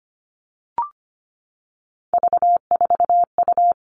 Morse code, heard as a keyed tone, about 25 wpm, 700 Hz. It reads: V4U
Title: All Cut Numbers - 3-Digit 25wpm